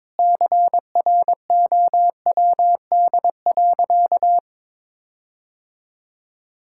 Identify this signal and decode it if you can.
Morse code, heard as a keyed tone, about 22 wpm, 700 Hz. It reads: CROWD.
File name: practice / 22wpm / Sentences with 4-Word Phrases - 22wpm